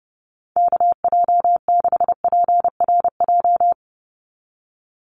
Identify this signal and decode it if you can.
Morse code, heard as a keyed tone, about 30 wpm, 700 Hz. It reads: KJ6PRJ
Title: USA Call Signs 30wpm